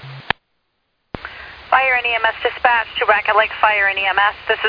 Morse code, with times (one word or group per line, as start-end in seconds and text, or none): none